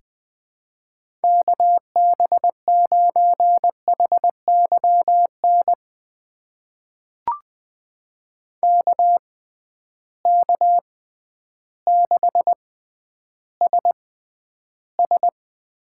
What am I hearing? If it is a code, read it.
Morse code, heard as a keyed tone, about 20 wpm, spaced as for 11 wpm, 700 Hz. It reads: KB9HYN KK6SS
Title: USA Call Signs - 6 Characters - 6x Character Spacing - 20wpm